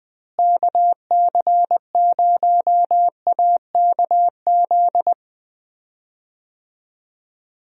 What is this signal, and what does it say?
Morse code, heard as a keyed tone, about 20 wpm, 700 Hz. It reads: KC0AKZ
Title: USA Call Signs - 6 Characters - Repeated 2x - 20wpm